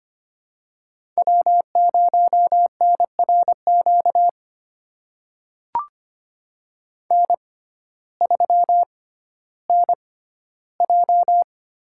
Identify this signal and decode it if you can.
Morse code, heard as a keyed tone, about 25 wpm, spaced as for 14 wpm, 700 Hz. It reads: W0NRQ N3NJ
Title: USA Call Signs - 5 Characters - 6x Character Spacing - 25wpm